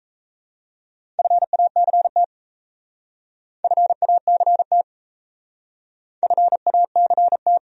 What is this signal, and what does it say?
Morse code, heard as a keyed tone, about 37 wpm, 700 Hz. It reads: FACT FACT FACT